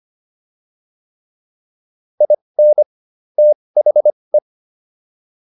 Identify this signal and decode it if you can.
Morse code, heard as a keyed tone, about 25 wpm, spaced as for 20 wpm, 600 Hz. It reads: IN THE